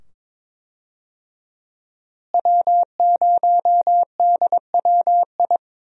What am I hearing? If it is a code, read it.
Morse code, heard as a keyed tone, about 22 wpm, 700 Hz. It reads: W0DWI
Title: USA Call Signs - 5 Characters - Repeated 2x - 22wpm